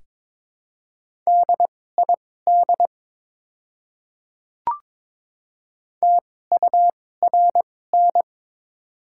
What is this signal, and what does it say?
Morse code, heard as a keyed tone, about 22 wpm, spaced as for 16 wpm, 700 Hz. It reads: DID TURN
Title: 2x Character Spacing - Top 200 Words - 22wpm